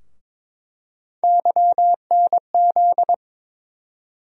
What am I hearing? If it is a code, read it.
Morse code, heard as a keyed tone, about 22 wpm, 700 Hz. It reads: YNZ
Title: USA Call Sign Postfixes - 3 Letters 22wpm